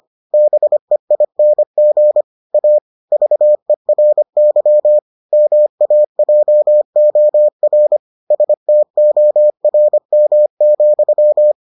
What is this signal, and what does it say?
Morse code, heard as a keyed tone, about 25 wpm, 600 Hz. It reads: BEING A VERY MAJOR STORM,